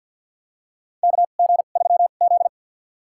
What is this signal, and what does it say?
Morse code, heard as a keyed tone, about 52 wpm, spaced as for 37 wpm, 690 Hz. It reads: KG3Z
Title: USA Call Signs - 2x Character Spacing - 50wpm